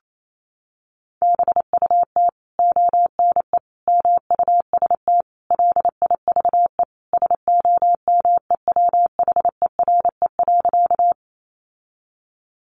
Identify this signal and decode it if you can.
Morse code, heard as a keyed tone, about 28 wpm, 700 Hz. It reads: BUT ONE MUST LIVE SOMEWHERE.
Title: Sentences from Top 1400 Words 28wpm